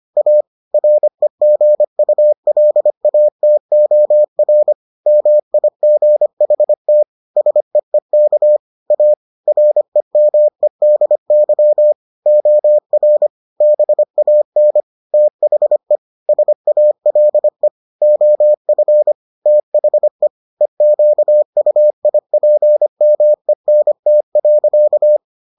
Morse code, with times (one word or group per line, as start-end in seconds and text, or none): 0.2-0.4 A
0.7-4.7 REGULATOR
5.1-7.0 MIGHT
7.4-8.6 SEEK
8.9-9.1 A
9.5-11.9 REMEDY
12.3-13.3 OR
13.6-14.8 BAN
15.1-16.0 THE
16.3-17.7 SALE
18.0-19.1 OF
19.5-20.3 THE
20.6-25.2 EQUIPMENT.